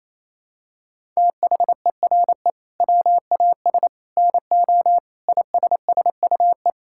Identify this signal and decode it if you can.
Morse code, heard as a keyed tone, about 28 wpm, 700 Hz. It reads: THERE WAS NO ISSUE
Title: Sentences from Top 1000 Words - Review Part 6 - 28wpm